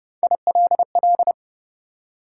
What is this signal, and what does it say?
Morse code, heard as a keyed tone, about 30 wpm, 700 Hz. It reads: ILL
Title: Three Letter Words - 30wpm